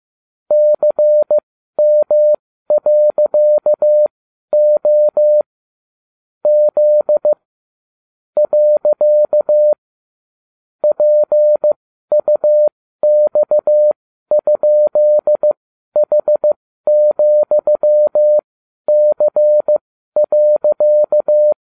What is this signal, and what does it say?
Morse code, heard as a keyed tone, about 15 wpm, spaced as for 12 wpm, 600 Hz. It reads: CM.O Z . PUX?H,C.